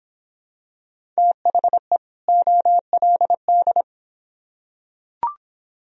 Morse code, heard as a keyed tone, about 24 wpm, 700 Hz. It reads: THE OLD